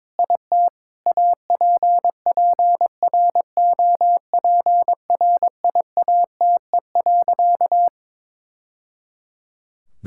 Morse code, heard as a keyed tone, about 22 wpm, 700 Hz. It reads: IT APPROPRIATE.
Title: Sentences from Top 1000 Words - Review Part 5 - 22wpm